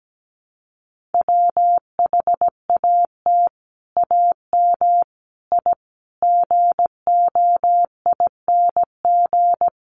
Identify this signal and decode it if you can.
Morse code, heard as a keyed tone, about 17 wpm, 700 Hz. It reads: WHAT AM I GOING
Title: Sets of 5 Words Encore 17wpm